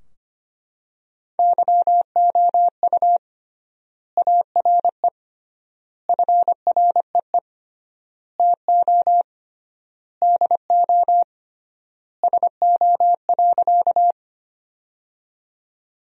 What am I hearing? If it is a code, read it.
Morse code, heard as a keyed tone, about 25 wpm, 700 Hz. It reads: YOU ARE FREE TO DO SO.
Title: 3x Word-Spacing - Sentences from Top 500 Words - 25wpm